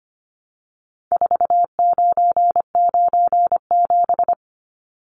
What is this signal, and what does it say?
Morse code, heard as a keyed tone, about 25 wpm, 700 Hz. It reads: 4997
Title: Numbers - 4-Digit 25wpm